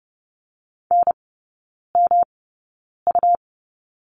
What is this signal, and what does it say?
Morse code, heard as a keyed tone, about 30 wpm, spaced as for 9 wpm, 700 Hz. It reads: NMU
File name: Lesson 18e - Letter m - Three Characters 30wpm